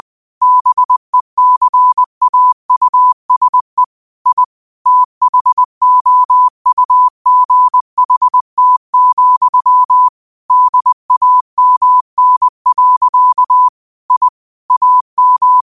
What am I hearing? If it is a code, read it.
Morse code, heard as a keyed tone, about 20 wpm, 1000 Hz. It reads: BECAUSE I THOUGHT, DAMN. I AM
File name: qotdfiles / 20wpm